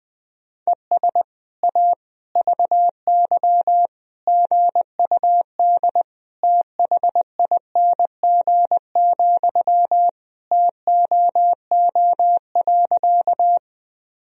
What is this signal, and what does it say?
Morse code, heard as a keyed tone, about 21 wpm, 700 Hz. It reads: ES A VY GUD THING, TOO.
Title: Sentences from Top 200 Words and Abbreviations 20wpm